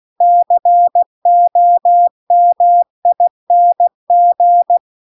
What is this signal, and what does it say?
Morse code, heard as a keyed tone, about 16 wpm, 700 Hz. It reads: COMING
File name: CW-The-Big-Twoheaded-River-2-16WPM